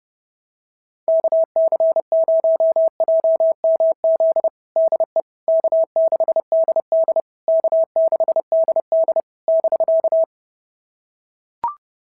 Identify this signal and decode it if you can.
Morse code, heard as a keyed tone, about 30 wpm, 660 Hz. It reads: KC0JMZ DE K6DD K6DD <BK>
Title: SOTA QSO Practice 30wpm